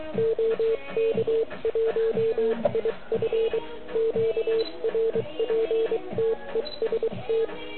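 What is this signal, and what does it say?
Morse code, heard as a keyed tone, about 23 wpm, 440 Hz, with noise. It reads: OK1IF QRPTEST
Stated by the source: Morse code from the OK QRP zavod 2018 contest, Chrudim.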